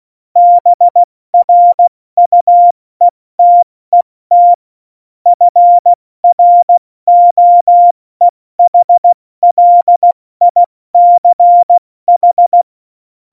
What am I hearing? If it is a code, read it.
Morse code, heard as a keyed tone, about 16 wpm, spaced as for 14 wpm, 700 Hz. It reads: BRUETET FROEHLICH
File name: Morsekurs / Witze / Witz 28 April 2020, 70 F80